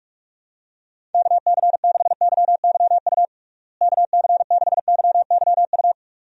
Morse code, heard as a keyed tone, about 46 wpm, 700 Hz. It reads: KC6YYU KC6YYU
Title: USA Call Signs - 6 Characters - Repeated 2x - 45wpm